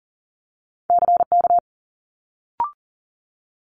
Morse code, heard as a keyed tone, about 41 wpm, 700 Hz. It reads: CK
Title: Multi-letter Phonemes Quick Fire 40wpm